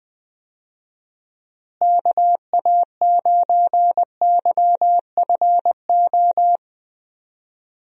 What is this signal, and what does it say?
Morse code, heard as a keyed tone, about 20 wpm, 700 Hz. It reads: KA9YFO
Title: USA Call Signs - 6 Characters - Repeated 2x - 20wpm